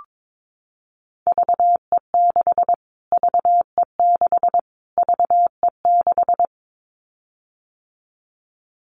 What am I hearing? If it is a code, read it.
Morse code, heard as a keyed tone, about 22 wpm, 700 Hz. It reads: VE6 VE6 VE6
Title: Callsign Prefixes for Top 25 DXCC - Repeated 3x - 22wpm